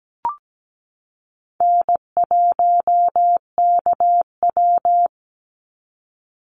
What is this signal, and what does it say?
Morse code, heard as a keyed tone, about 17 wpm, 700 Hz. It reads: N1KW
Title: Call Signs Repeated 2x - 17wpm